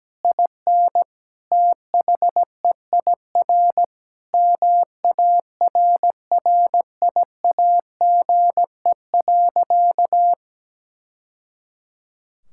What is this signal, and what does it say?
Morse code, heard as a keyed tone, about 17 wpm, 700 Hz. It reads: IN THEIR MARRIAGE.
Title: Sentences from Top 1000 Words - Review Part 6 - 17wpm